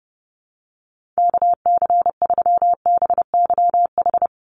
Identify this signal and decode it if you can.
Morse code, heard as a keyed tone, about 30 wpm, 700 Hz. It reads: KC3BYH
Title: USA Call Signs 30wpm